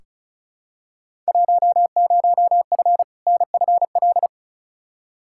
Morse code, heard as a keyed tone, about 36 wpm, 700 Hz. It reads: 10F NFL
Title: Field Day Exchanges Encore - 35wpm